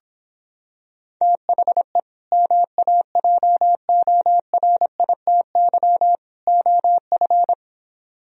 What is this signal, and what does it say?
Morse code, heard as a keyed tone, about 26 wpm, 700 Hz. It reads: THE MAJORITY OF